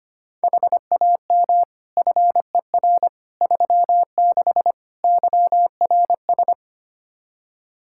Morse code, heard as a keyed tone, about 25 wpm, 700 Hz. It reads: HAM FER 36 YRS
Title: K1USN SST QSO Practice 25wpm